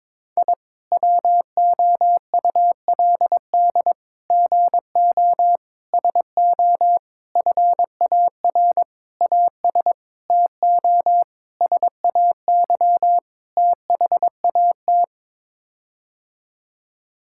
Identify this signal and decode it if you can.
Morse code, heard as a keyed tone, about 22 wpm, 700 Hz. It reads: I WOULD GO SO FAR AS TO SAY THAT
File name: Sets of 9 Words - 22wpm